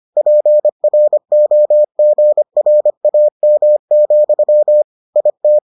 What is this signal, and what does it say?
Morse code, heard as a keyed tone, about 25 wpm, 600 Hz. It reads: PROGRAM, IT